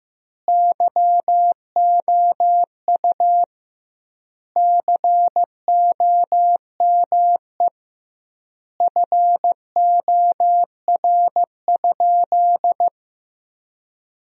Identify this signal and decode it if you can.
Morse code, heard as a keyed tone, about 14 wpm, 700 Hz. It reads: YOU COME FOR?
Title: 2x Word-Spacing - Sentences from Top 100 Words - 15wpm